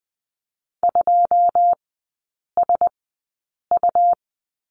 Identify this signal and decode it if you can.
Morse code, heard as a keyed tone, about 20 wpm, spaced as for 8 wpm, 700 Hz. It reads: 2SU
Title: 100 Lesson 17e - Letter u - Three Characters 20wpm